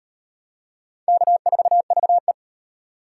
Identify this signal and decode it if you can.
Morse code, heard as a keyed tone, about 38 wpm, 700 Hz. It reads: K4VE